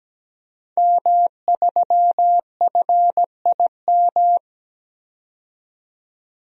Morse code, heard as a keyed tone, about 17 wpm, 700 Hz. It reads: M3FIM
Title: DXCC-100 Call Signs - 5 Characters - 17wpm